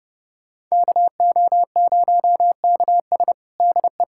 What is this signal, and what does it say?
Morse code, heard as a keyed tone, about 30 wpm, 700 Hz. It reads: KO0KS DE
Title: USA Callsign DE Callsign 30wpm